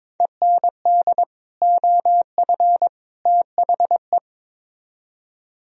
Morse code, heard as a keyed tone, about 25 wpm, 700 Hz. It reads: END OF THE
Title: Speed-Racing - Sets of 3 Words 22wpm